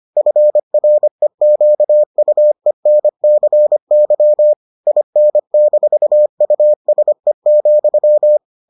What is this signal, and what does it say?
Morse code, heard as a keyed tone, about 25 wpm, 600 Hz. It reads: FREQUENCY IN-USE,